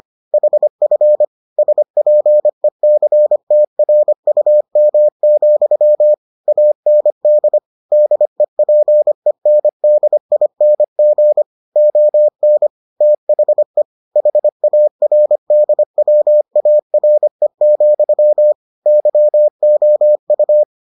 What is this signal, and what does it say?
Morse code, heard as a keyed tone, about 25 wpm, 600 Hz. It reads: HF SPECTRUM, AND DEPENDING ON THE HARDWARE, YOU